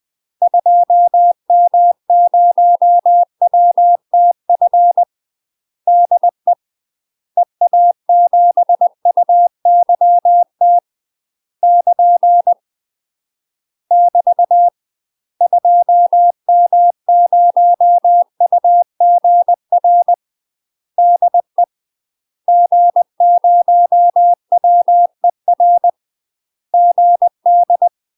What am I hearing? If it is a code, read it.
Morse code, heard as a keyed tone, about 20 wpm, 700 Hz. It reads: 2M0WTF DE EA7UYT <KN> <BT> 2M0UGR DE G0WER GD
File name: UK MASTER QSOs TRAINING / 20WPM / UK MASTER QSOs 0069